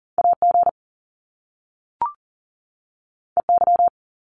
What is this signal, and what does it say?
Morse code, heard as a keyed tone, about 41 wpm, 700 Hz. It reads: AG EY
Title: Top Two Letter Combinations in Words 40wpm